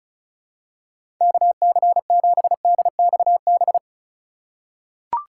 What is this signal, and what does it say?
Morse code, heard as a keyed tone, about 34 wpm, 700 Hz. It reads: KC7DXB
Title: USA Call Signs - 3x Character Spacing - 35wpm